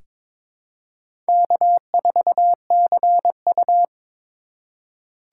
Morse code, heard as a keyed tone, about 22 wpm, 700 Hz. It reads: K4CU
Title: USA Call Signs - 4 Characters - 3x Character Spacing - 22wpm